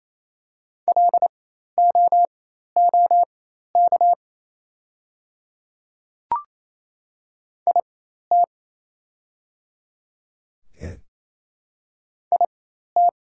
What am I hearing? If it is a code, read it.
Morse code, heard as a keyed tone, about 27 wpm, spaced as for 13 wpm, 700 Hz. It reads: LOOK IT IT
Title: 4x Character Spacing - Top 200 Words - 28wpm